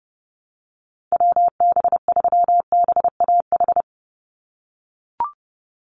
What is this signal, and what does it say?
Morse code, heard as a keyed tone, about 30 wpm, 700 Hz. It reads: WB3BAH
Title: USA Call Signs 30wpm